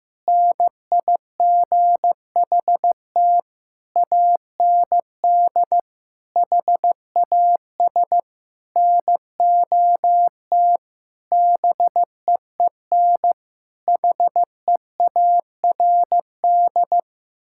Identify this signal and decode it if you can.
Morse code, heard as a keyed tone, about 15 wpm, 700 Hz. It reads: NIGHT AND HAS NOT BEEN HEARD
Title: Sentences from Top 1600 Words 15wpm